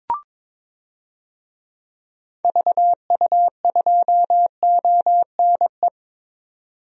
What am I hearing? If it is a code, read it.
Morse code, heard as a keyed tone, about 22 wpm, 700 Hz. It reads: VU2ONE
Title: DXCC-100 Call Signs - 6 Characters - 22wpm